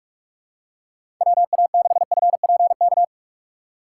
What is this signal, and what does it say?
Morse code, heard as a keyed tone, about 46 wpm, 700 Hz. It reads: WA6FPK
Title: USA Call Signs - 6 Characters - 2x Character Spacing - 45wpm